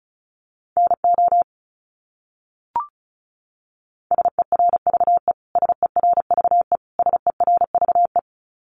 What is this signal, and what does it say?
Morse code, heard as a keyed tone, about 34 wpm, 700 Hz. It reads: NO SERVE SERVE SERVE
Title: Top 500 Words Repeated 35wpm